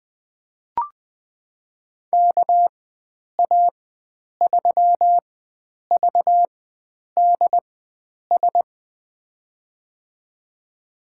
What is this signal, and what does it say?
Morse code, heard as a keyed tone, about 20 wpm, spaced as for 9 wpm, 700 Hz. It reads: KA3VDS
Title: USA Call Signs - 4x Character Spacing - 20wpm